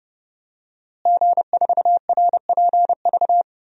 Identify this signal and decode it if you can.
Morse code, heard as a keyed tone, about 30 wpm, 700 Hz. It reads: G4RPV
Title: DXCC-100 Call Signs - 5 Characters - Repeated 3x - 30wpm